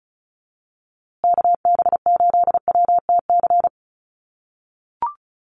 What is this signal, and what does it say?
Morse code, heard as a keyed tone, about 34 wpm, 700 Hz. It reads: KB8WTC
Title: USA Call Signs Repeated 2x 35wpm